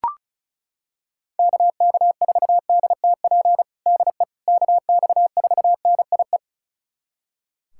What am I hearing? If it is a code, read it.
Morse code, heard as a keyed tone, about 36 wpm, 700 Hz. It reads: KK4DTP DE KX4NIE
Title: USA Callsign DE Callsign - 6 Characters - 35wpm